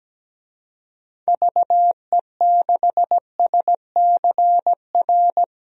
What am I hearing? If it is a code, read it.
Morse code, heard as a keyed tone, about 17 wpm, 700 Hz. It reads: VE6SCR
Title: DXCC-100 Call Signs - 6 Characters - Repeated 3x - 17wpm